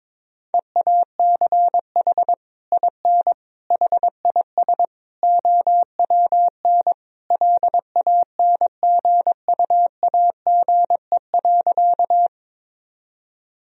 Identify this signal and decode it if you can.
Morse code, heard as a keyed tone, about 22 wpm, 700 Hz. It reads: EACH IN HIS OWN LANGUAGE.
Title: Sentences from Top 1000 Words - Review Part 3 - 22wpm